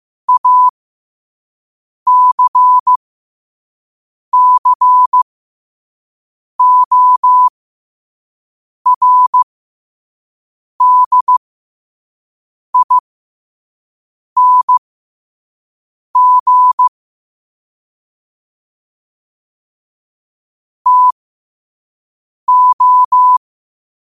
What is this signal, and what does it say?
Morse code, heard as a keyed tone, about 15 wpm, spaced as for 5 wpm, 1000 Hz. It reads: ACCORDING TO